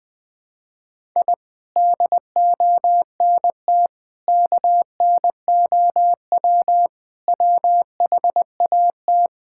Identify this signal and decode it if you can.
Morse code, heard as a keyed tone, about 21 wpm, 700 Hz. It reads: I DONT KNOW WHAT
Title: Sentences with 8-Word Phrases - 20wpm